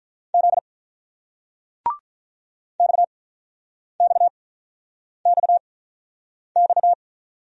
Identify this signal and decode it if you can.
Morse code, heard as a keyed tone, about 49 wpm, spaced as for 9 wpm, 700 Hz. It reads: Z XXXX